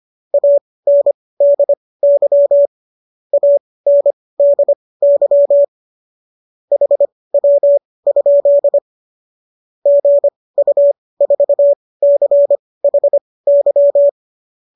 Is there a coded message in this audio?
Yes: Morse code from 0.3 to 14.1 s, heard as a keyed tone, about 25 wpm, spaced as for 18 wpm, 570 Hz. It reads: ANDY ANDY HW? GU4CHY